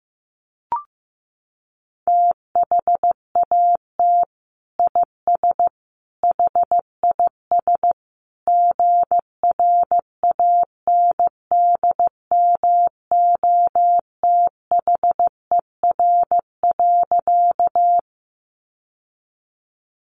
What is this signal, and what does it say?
Morse code, heard as a keyed tone, about 15 wpm, 700 Hz. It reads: THAT IS HIS GRANDMOTHER.
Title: Sentences from Top 1200 Words 15wpm